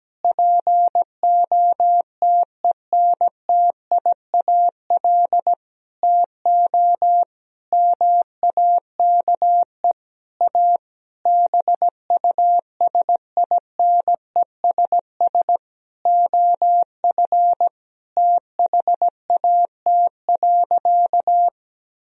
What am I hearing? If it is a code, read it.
Morse code, heard as a keyed tone, about 17 wpm, 700 Hz. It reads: POTENTIAL TO MAKE A BUSINESS OF THAT.
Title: Sentences from Top 1000 Words - Review Part 4 - 17wpm